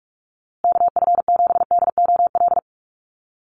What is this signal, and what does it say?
Morse code, heard as a keyed tone, about 44 wpm, 700 Hz. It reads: KF7DOL
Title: USA Call Signs Repeated 2x 45wpm